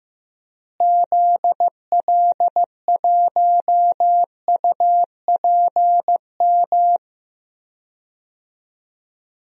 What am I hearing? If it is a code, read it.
Morse code, heard as a keyed tone, about 15 wpm, 700 Hz. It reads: ZL1UPM
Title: DXCC-100 Call Signs - 6 Characters - Repeated 3x - 15wpm